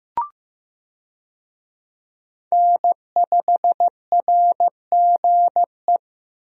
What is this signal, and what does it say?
Morse code, heard as a keyed tone, about 15 wpm, 700 Hz. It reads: N5RGE
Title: USA Call Signs - 5 Characters - Repeated 2x - 15wpm